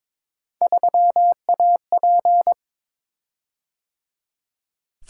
Morse code, heard as a keyed tone, about 22 wpm, 700 Hz. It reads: 3AP
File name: practice / 22wpm / Three Letter-Number Combinations - 22wpm